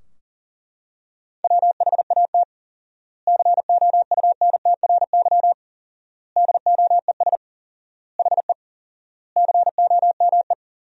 Morse code, heard as a keyed tone, about 40 wpm, 700 Hz. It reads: WHAT COUNTRY DOES HE COME
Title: 4x Word-Spacing - Sentences from Top 300 Words - 40wpm